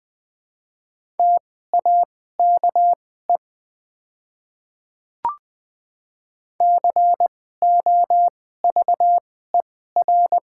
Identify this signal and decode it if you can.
Morse code, heard as a keyed tone, about 20 wpm, spaced as for 14 wpm, 700 Hz. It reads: TAKE COVER
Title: 2x Character Spacing - Top 300 Words - 20wpm